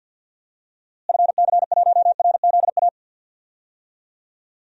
Morse code, heard as a keyed tone, about 47 wpm, 700 Hz. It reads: FC1RZA
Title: DXCC-100 Call Signs - 6 Characters - Repeated 2x - 50wpm